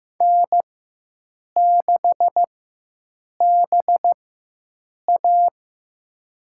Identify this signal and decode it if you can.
Morse code, heard as a keyed tone, about 15 wpm, spaced as for 7 wpm, 700 Hz. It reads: N6BA
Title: USA Call Signs - 4x Character Spacing - 15wpm